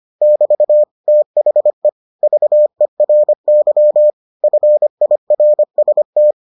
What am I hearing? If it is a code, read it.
Morse code, heard as a keyed tone, about 25 wpm, 600 Hz. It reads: <BT> THE VERY FIRST